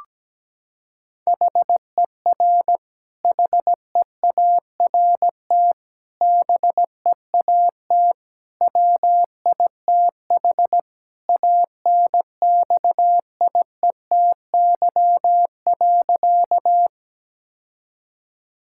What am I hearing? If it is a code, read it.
Morse code, heard as a keyed tone, about 17 wpm, 700 Hz. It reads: HER HEART BEAT WITH ANXIETY.
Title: Sentences from Top 1600 Words 17wpm